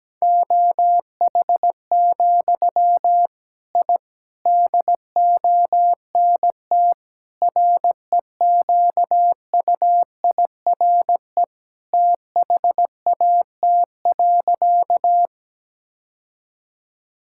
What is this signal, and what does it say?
Morse code, heard as a keyed tone, about 17 wpm, 700 Hz. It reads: OH, I DONT REQUIRE THAT.
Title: Sentences from Top 1000 Words - Review Part 6 - 17wpm